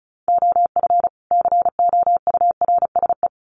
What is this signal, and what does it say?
Morse code, heard as a keyed tone, about 34 wpm, 700 Hz. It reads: OF COURSE
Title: Sets of 2 Words 35wpm